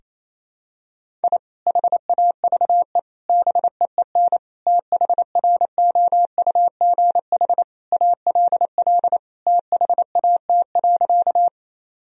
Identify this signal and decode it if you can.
Morse code, heard as a keyed tone, about 28 wpm, 700 Hz. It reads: I HAVE BEEN THROUGH ALL THAT.
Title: Sentences from Top 1000 Words - Review Part 1 - 28wpm